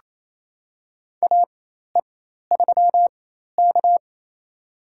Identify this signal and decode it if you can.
Morse code, heard as a keyed tone, about 28 wpm, spaced as for 13 wpm, 700 Hz. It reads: AE3K
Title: USA Call Signs - 4 Characters - 4x Character Spacing - 28wpm